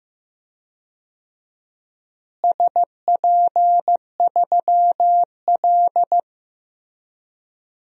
Morse code, heard as a keyed tone, about 15 wpm, 700 Hz. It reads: SP3L
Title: DXCC-100 Call Signs - 4 Characters - Repeated 2x - 15wpm